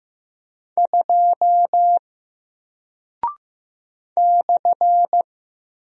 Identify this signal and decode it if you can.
Morse code, heard as a keyed tone, about 15 wpm, 700 Hz. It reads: 2 /
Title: Single Character (Letter:Number:Punctuation:Prosign) Alt 15wpm